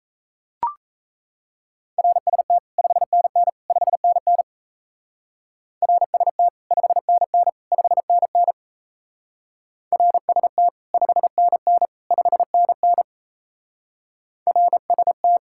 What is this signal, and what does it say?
Morse code, heard as a keyed tone, about 35 wpm, 700 Hz. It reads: RST 5NN 5NN RST 5NN 5NN RST 5NN 5NN RST
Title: Speed-Racing Common QSO Phrases - 28wpm